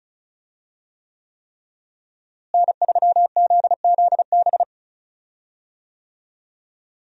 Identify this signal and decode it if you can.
Morse code, heard as a keyed tone, about 36 wpm, 700 Hz. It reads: N3ZZB
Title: USA Call Signs - 5 Characters - Repeated 3x - 35wpm